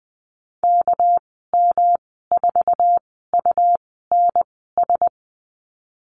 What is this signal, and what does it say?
Morse code, heard as a keyed tone, about 20 wpm, spaced as for 15 wpm, 700 Hz. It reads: KM4UNS
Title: 238 Lesson 38g - Letter q - Call Signs 20wpm